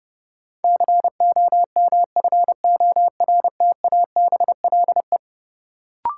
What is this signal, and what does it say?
Morse code, heard as a keyed tone, about 31 wpm, 700 Hz. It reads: COMFORTABLE
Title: Sentences from Top 1000 Words - Review Part 7 - New Word Drill - 30wpm